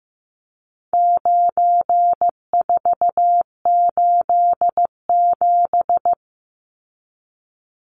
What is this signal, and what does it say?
Morse code, heard as a keyed tone, about 15 wpm, 700 Hz. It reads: 9487
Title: Numbers - 4-Digit 15wpm